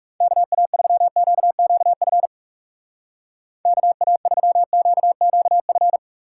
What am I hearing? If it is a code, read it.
Morse code, heard as a keyed tone, about 39 wpm, 700 Hz. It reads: KA3QQF KA3QQF